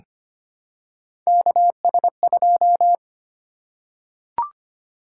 Morse code, heard as a keyed tone, about 25 wpm, 700 Hz. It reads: KS2
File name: Three Letter-Number Combinations - 25wpm